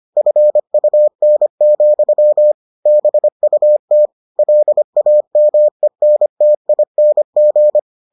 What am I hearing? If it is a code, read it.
Morse code, heard as a keyed tone, about 25 wpm, 600 Hz. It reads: FUN, BUT LAMENTING